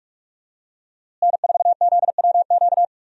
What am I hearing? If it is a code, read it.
Morse code, heard as a keyed tone, about 45 wpm, 700 Hz. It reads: N4ZWQ
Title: USA Call Signs - 5 Characters - Repeated 3x - 45wpm